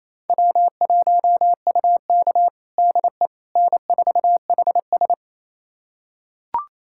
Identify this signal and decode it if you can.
Morse code, heard as a keyed tone, about 28 wpm, 700 Hz. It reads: W1UK DE N4HS
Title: USA Callsign DE Callsign - 4 Characters - 28wpm